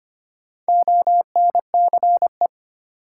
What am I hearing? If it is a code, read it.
Morse code, heard as a keyed tone, about 25 wpm, 700 Hz. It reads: ONCE
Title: Sentences from Top 300 Words - New Word Drill - 25wpm